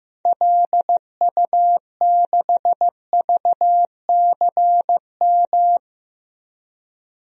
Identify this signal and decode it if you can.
Morse code, heard as a keyed tone, about 15 wpm, 700 Hz. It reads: LU6VCM
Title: DXCC-100 Call Signs - 6 Characters - 15wpm